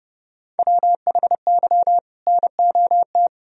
Morse code, heard as a keyed tone, about 30 wpm, 700 Hz. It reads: WHY NOT
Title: Sentences from Top 1000 Words - Review Part 3 - 30wpm